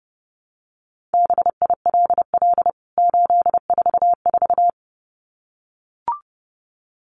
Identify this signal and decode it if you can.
Morse code, heard as a keyed tone, about 30 wpm, 700 Hz. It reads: BILL 844
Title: CWT Exchange 30wpm